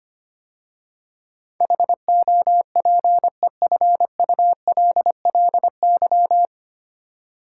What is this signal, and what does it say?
Morse code, heard as a keyed tone, about 25 wpm, 700 Hz. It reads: HOPEFULLY
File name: practice / 25wpm / 9 Letter Words - First 3 Letters Given - 25wpm